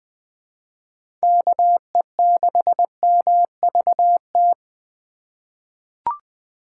Morse code, heard as a keyed tone, about 20 wpm, 700 Hz. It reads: KE6MVT
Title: USA Call Signs - 6 Characters - 3x Character Spacing - 20wpm